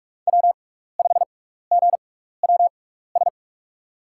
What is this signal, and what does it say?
Morse code, heard as a keyed tone, about 44 wpm, spaced as for 15 wpm, 690 Hz. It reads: W5GWS
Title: USA Call Signs - 5 Characters - 6x Character Spacing - 45wpm